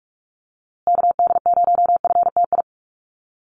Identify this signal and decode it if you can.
Morse code, heard as a keyed tone, about 44 wpm, 700 Hz. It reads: KD0FTI
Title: USA Call Signs Repeated 3x 45wpm